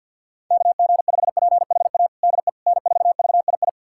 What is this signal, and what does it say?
Morse code, heard as a keyed tone, about 52 wpm, 700 Hz. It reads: KG5PHA DE N4VII
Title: USA Callsign DE Callsign 50wpm